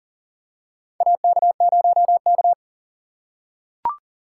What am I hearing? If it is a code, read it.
Morse code, heard as a keyed tone, about 39 wpm, 700 Hz. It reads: AK0K
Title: USA Call Signs - 4 Characters - 4x Character Spacing - 40wpm